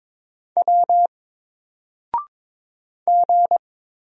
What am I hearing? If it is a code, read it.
Morse code, heard as a keyed tone, about 22 wpm, 700 Hz. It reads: W G
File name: Single-Letter-Number-22wpm